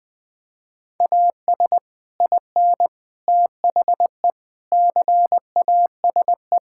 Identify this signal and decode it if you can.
Morse code, heard as a keyed tone, about 20 wpm, 700 Hz. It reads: AS IN THE CASE
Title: Build A Phrase - 5-Word Phrases - 20wpm